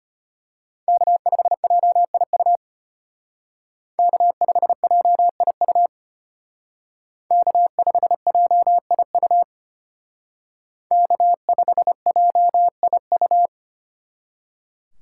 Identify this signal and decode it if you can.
Morse code, heard as a keyed tone, about 32 wpm, 700 Hz. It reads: K5JIU K5JIU K5JIU K5JIU